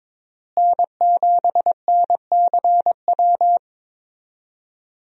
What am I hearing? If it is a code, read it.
Morse code, heard as a keyed tone, about 21 wpm, 700 Hz. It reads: N7NCW
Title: USA Call Signs - 5 Characters - Repeated 2x - 22wpm